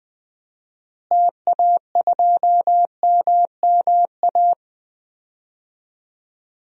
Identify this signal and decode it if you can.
Morse code, heard as a keyed tone, about 20 wpm, 700 Hz. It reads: TA2MMA
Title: DXCC-100 Call Signs - 6 Characters - Repeated 2x - 20wpm